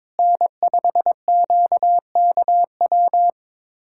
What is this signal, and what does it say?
Morse code, heard as a keyed tone, about 22 wpm, 700 Hz. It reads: N5QKW
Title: USA Callsign DE Callsign - 5 Characters - 22wpm